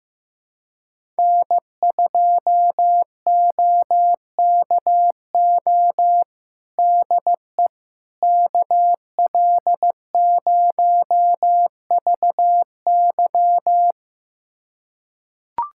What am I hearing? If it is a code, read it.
Morse code, heard as a keyed tone, about 15 wpm, 700 Hz. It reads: N2OKO DE KL0VY
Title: USA Callsign DE Callsign 15wpm